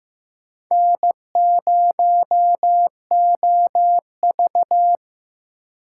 Morse code, heard as a keyed tone, about 18 wpm, 700 Hz. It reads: N0OV
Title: Speed Racing - USA 1x2 Call Signs 15wpm